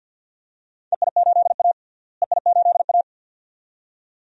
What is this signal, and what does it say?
Morse code, heard as a keyed tone, about 50 wpm, 700 Hz. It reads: EI8A EI8A
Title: DXCC-100 Call Signs - 4 Characters - Repeated 2x - 50wpm